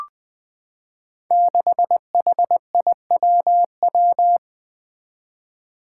Morse code, heard as a keyed tone, about 20 wpm, 700 Hz. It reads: 6HIWW
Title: Five Challenging Letter-Number Combinations 20wpm